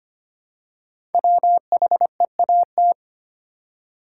Morse code, heard as a keyed tone, about 25 wpm, 700 Hz. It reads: WHEAT
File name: 5 Letter Words - First 3 Letters Given - 25wpm